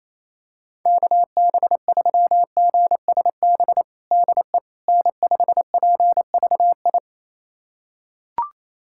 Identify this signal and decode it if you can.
Morse code, heard as a keyed tone, about 28 wpm, 700 Hz. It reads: KB3GSB DE N5PVI
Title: USA Callsign DE Callsign 28wpm